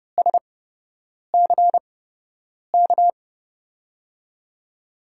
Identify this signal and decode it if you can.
Morse code, heard as a keyed tone, about 30 wpm, spaced as for 8 wpm, 700 Hz. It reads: SCK
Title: USA Call Signs - 6 Characters - 8x Character Spacing - 30wpm